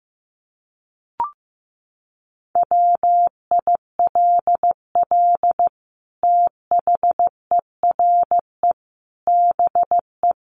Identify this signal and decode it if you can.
Morse code, heard as a keyed tone, about 15 wpm, 700 Hz. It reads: WILL THERE BE
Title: Sentences from Top 1300 Words 15wpm